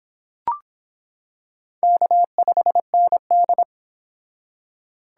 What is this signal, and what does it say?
Morse code, heard as a keyed tone, about 26 wpm, 700 Hz. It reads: K5ND